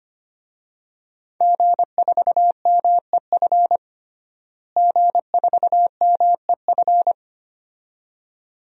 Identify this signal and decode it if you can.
Morse code, heard as a keyed tone, about 25 wpm, 700 Hz. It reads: G4MEF G4MEF
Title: DXCC-100 Call Signs - 5 Characters - Repeated 2x - 25wpm